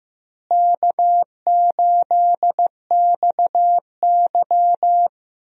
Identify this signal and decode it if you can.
Morse code, heard as a keyed tone, about 15 wpm, 700 Hz. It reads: K8XY
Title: USA Call Signs - 4 Characters - Repeated 3x - 15wpm